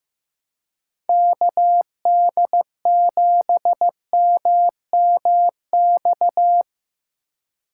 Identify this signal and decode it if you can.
Morse code, heard as a keyed tone, about 15 wpm, 700 Hz. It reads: KD7MMX
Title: USA Call Signs - 6 Characters - 2x Character Spacing - 15wpm